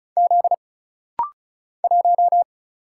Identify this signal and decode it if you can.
Morse code, heard as a keyed tone, about 36 wpm, 700 Hz. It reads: Z 1
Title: Single Letter-Number - Rapid-Fire 35wpm